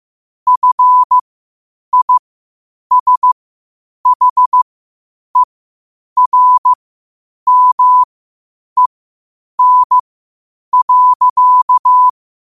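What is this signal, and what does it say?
Morse code, heard as a keyed tone, about 15 wpm, spaced as for 8 wpm, 1000 Hz. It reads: FISHERMEN.
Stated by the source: Morse code at 8 wpm.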